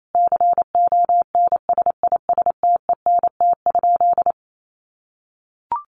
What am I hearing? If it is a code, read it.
Morse code, heard as a keyed tone, about 28 wpm, 700 Hz. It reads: CONSISTENT?
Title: Sentences from Top 1500 Words 28wpm